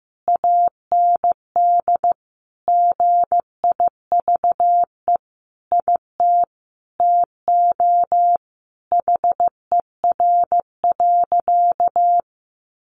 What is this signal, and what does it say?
Morse code, heard as a keyed tone, about 15 wpm, 700 Hz. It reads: AND GIVE IT TO HER.
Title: Sentences from Top 1400 Words 15wpm